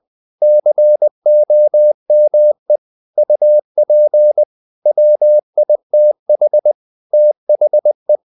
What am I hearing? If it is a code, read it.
Morse code, heard as a keyed tone, about 20 wpm, 600 Hz. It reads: COME UP WITH THE
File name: the old man and the sea / the old man and the sea part 1 20wpm